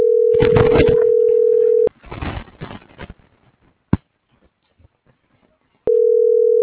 Ringback tone (North America), 440 + 480 Hz.